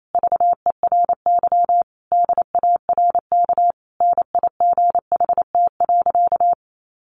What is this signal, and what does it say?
Morse code, heard as a keyed tone, about 28 wpm, 700 Hz. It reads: VERY DARK NIGHT.
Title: Sentences from Top 500 Words 28wpm